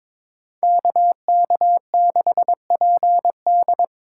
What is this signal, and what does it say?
Morse code, heard as a keyed tone, about 21 wpm, 700 Hz. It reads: KK6PD
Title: USA Call Signs - 5 Characters - 3x Character Spacing - 22wpm